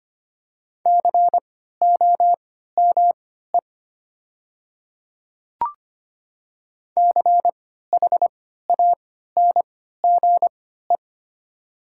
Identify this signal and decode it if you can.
Morse code, heard as a keyed tone, about 25 wpm, spaced as for 14 wpm, 700 Hz. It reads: COME CHANGE
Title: 3x Character Spacing - Top 300 Words - 25wpm